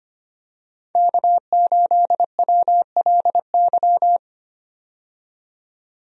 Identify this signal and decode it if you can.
Morse code, heard as a keyed tone, about 25 wpm, 700 Hz. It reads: K8WLY
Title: USA Call Signs - 5 Characters - 25wpm